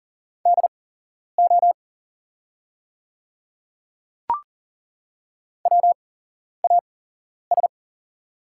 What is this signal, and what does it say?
Morse code, heard as a keyed tone, about 40 wpm, spaced as for 11 wpm, 690 Hz. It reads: DO WAS
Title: 8x Character Spacing - Top 200 Words - 40wpm